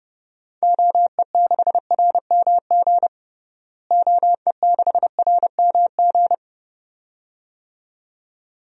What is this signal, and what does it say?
Morse code, heard as a keyed tone, about 32 wpm, 700 Hz. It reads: OE6RMG OE6RMG
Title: DXCC-100 Call Signs - 6 Characters - Repeated 2x - 30wpm